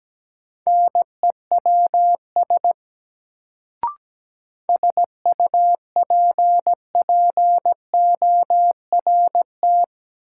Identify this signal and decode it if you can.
Morse code, heard as a keyed tone, about 17 wpm, 700 Hz. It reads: NEWS SUPPORT
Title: Sentences from Top 800 Words - New Word Drill - 17wpm